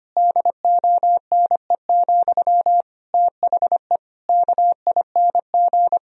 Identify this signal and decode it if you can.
Morse code, heard as a keyed tone, about 25 wpm, 700 Hz. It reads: DONE, THE KING
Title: Sentences from Top 1000 Words - Review Part 3 - 25wpm